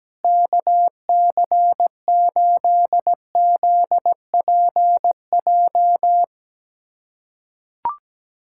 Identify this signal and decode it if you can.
Morse code, heard as a keyed tone, about 17 wpm, 700 Hz. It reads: KC8ZPJ
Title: USA Callsign DE Callsign - 6 Characters - 17wpm